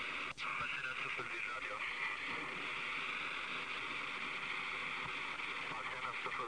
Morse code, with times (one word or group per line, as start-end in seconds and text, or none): none